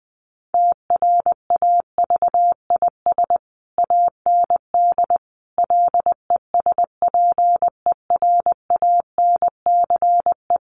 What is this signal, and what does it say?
Morse code, heard as a keyed tone, about 20 wpm, 700 Hz. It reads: TRAVIS AND LESPERANCE
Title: A Sound Of Thunder 20wpm